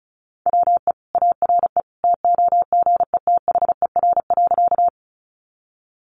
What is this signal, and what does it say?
Morse code, heard as a keyed tone, about 34 wpm, 700 Hz. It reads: WE ARE TOGETHER.
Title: Sentences from Top 300 Words 35wpm